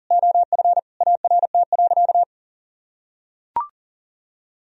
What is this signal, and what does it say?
Morse code, heard as a keyed tone, about 40 wpm, 700 Hz. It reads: OF ART.
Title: Sentences from Top 1000 Words - Review Part 4 - 40wpm